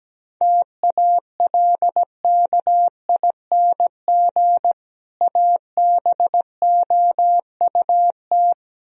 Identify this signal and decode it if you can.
Morse code, heard as a keyed tone, about 17 wpm, 700 Hz. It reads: TALKING ABOUT
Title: Speed-Racing - Sets of 2 Words 17wpm